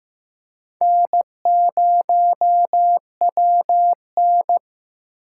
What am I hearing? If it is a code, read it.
Morse code, heard as a keyed tone, about 15 wpm, 700 Hz. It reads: N0WN
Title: USA Call Signs - 4 Characters - 3x Character Spacing - 15wpm